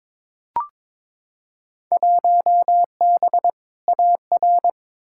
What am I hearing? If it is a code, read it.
Morse code, heard as a keyed tone, about 22 wpm, 700 Hz. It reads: 1B AR
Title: Field Day Exchanges Encore - 22wpm